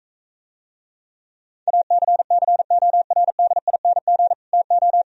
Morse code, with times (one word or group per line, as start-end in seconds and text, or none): 1.7-4.3 ACCORDING
4.5-5.0 TO